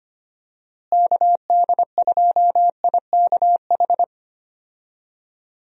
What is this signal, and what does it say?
Morse code, heard as a keyed tone, about 25 wpm, 700 Hz. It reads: KD2IKH
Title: USA Call Signs - 6 Characters - Repeated 2x - 25wpm